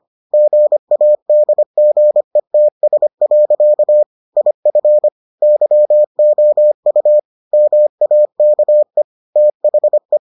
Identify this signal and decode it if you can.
Morse code, heard as a keyed tone, about 25 wpm, 600 Hz. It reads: GADGETS. IF YOU MAKE THE